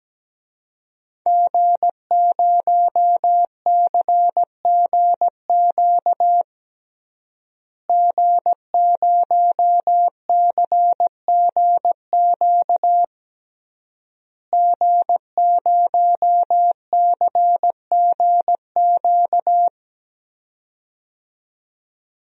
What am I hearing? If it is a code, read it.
Morse code, heard as a keyed tone, about 17 wpm, 700 Hz. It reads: G0CGQ G0CGQ G0CGQ
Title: DXCC-100 Call Signs - 5 Characters - Repeated 3x - 17wpm